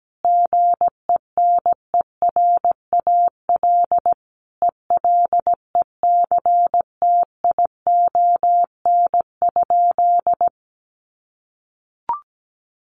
Morse code, heard as a keyed tone, about 17 wpm, 700 Hz. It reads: GENERAL ELECTION?
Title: Sentences from Top 1500 Words 17wpm